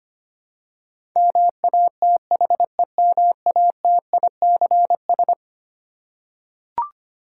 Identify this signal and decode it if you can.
Morse code, heard as a keyed tone, about 25 wpm, 700 Hz. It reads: MATHEMATICS
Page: morse-code-ninja-practice-25wpm